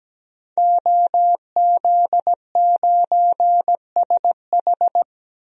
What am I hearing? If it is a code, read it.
Morse code, heard as a keyed tone, about 17 wpm, 700 Hz. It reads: OZ9SH
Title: DXCC-100 Call Signs - 5 Characters - 17wpm